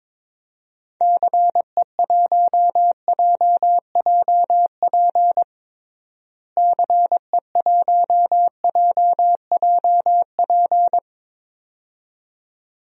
Morse code, heard as a keyed tone, about 21 wpm, 700 Hz. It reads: CE1JJP CE1JJP
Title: DXCC-100 Call Signs - 6 Characters - Repeated 2x - 22wpm